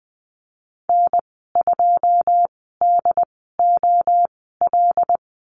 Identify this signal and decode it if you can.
Morse code, heard as a keyed tone, about 20 wpm, spaced as for 15 wpm, 700 Hz. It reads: N2DOL
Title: 088 Lesson 15g - Number 5 - Call Signs 20wpm